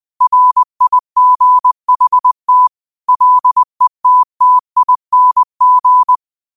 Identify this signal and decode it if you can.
Morse code, heard as a keyed tone, about 20 wpm, 1000 Hz. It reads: RIGHT LETTING
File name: qotdfiles / 20wpm